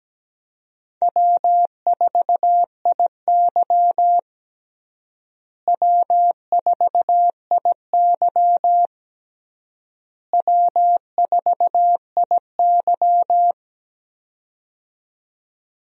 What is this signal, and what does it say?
Morse code, heard as a keyed tone, about 17 wpm, 700 Hz. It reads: W4IY W4IY W4IY
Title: USA Call Signs - 4 Characters - Repeated 3x - 17wpm